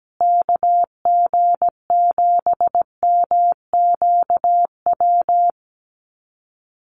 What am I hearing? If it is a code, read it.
Morse code, heard as a keyed tone, about 17 wpm, 700 Hz. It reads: KG7MQW
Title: USA Call Signs Repeated 2x 17wpm